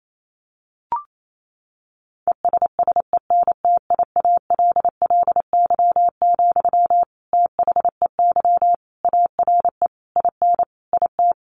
Morse code, heard as a keyed tone, about 28 wpm, 700 Hz. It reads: ESSENTIALLY, THEY ARE IN IT
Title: Sentences from Top 1300 Words 28wpm